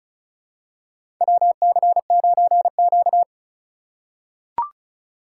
Morse code, heard as a keyed tone, about 34 wpm, 700 Hz. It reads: WC9Q
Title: USA Call Signs - 4 Characters - 4x Character Spacing - 35wpm